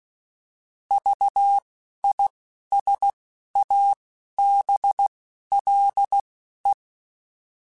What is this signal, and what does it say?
Morse code, heard as a keyed tone, about 16 wpm, 800 Hz. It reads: V I S A B L E
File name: Alphabetical / Alphabetical mp3s / Vs @10